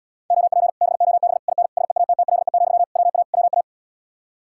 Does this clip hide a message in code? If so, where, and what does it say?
Yes: Morse code from 0.3 to 3.6 s, heard as a keyed tone, about 80 wpm, 700 Hz. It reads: OP BOB IN SEATTLE, WA WA